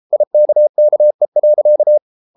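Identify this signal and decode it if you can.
Morse code, heard as a keyed tone, about 33 wpm, 600 Hz. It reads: IKKE.
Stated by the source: Morse code at 33 wpm, QRQ.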